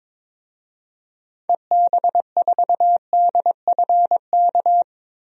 Morse code, heard as a keyed tone, about 22 wpm, 700 Hz. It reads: EB4DFK
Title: DXCC-100 Call Signs - 6 Characters - 22wpm